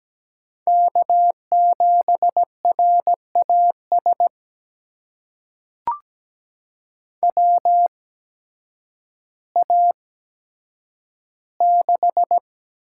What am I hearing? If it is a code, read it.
Morse code, heard as a keyed tone, about 17 wpm, spaced as for 9 wpm, 700 Hz. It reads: K7RAS WA6
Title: USA Call Signs - 8x Character Spacing - 17wpm